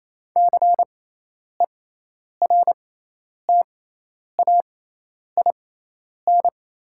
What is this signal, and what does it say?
Morse code, heard as a keyed tone, about 28 wpm, spaced as for 10 wpm, 700 Hz. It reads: CERTAIN
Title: 6x Character Spacing - Top 500 Words - 28wpm